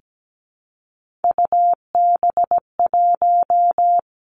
Morse code, heard as a keyed tone, about 17 wpm, 700 Hz. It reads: UB1
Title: Three Challenging Letter-Number Combinations 17wpm